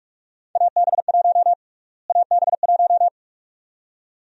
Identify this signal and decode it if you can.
Morse code, heard as a keyed tone, about 47 wpm, 700 Hz. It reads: AB1 AB1
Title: USA Call Sign Prefixes - Repeated 2x - 45wpm